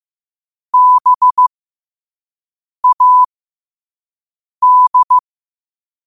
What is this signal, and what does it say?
Morse code, heard as a keyed tone, about 15 wpm, spaced as for 5 wpm, 1000 Hz. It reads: BAD
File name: qotdfiles / 5wpm